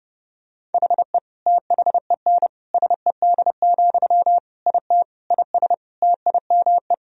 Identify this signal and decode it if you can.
Morse code, heard as a keyed tone, about 30 wpm, 700 Hz. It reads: HE THEN SED, IT IS TIME
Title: Sentences from Top 300 Words and Abbreviations 30wpm